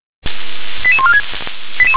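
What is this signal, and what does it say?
Five-tone selective call (ZVEI), heard as quick runs of five tones, tones of about 70 ms. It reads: 88146
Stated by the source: ZVEI five-tone selective call received via SDR.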